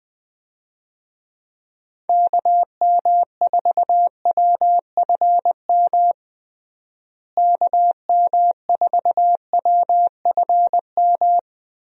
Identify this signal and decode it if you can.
Morse code, heard as a keyed tone, about 20 wpm, 700 Hz. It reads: KM4WFM KM4WFM
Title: USA Call Signs - 6 Characters - Repeated 3x - 20wpm